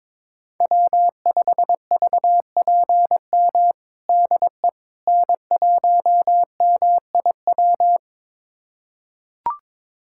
Morse code, heard as a keyed tone, about 22 wpm, 700 Hz. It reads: W5VPM DE N1MIW
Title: USA Callsign DE Callsign - 5 Characters - 22wpm